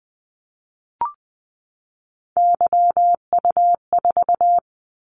Morse code, heard as a keyed tone, about 20 wpm, 700 Hz. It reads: YU4
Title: Callsign Prefixes for Top 25 DXCC - 20wpm